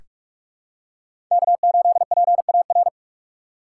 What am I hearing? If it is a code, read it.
Morse code, heard as a keyed tone, about 43 wpm, 700 Hz. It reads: K8PAR
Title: USA Call Signs - 5 Characters - 2x Character Spacing - 45wpm